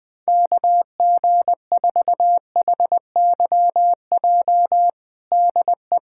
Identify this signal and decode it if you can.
Morse code, heard as a keyed tone, about 21 wpm, 700 Hz. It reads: KG4HYJ DE
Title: USA Callsign DE Callsign - 6 Characters - 20wpm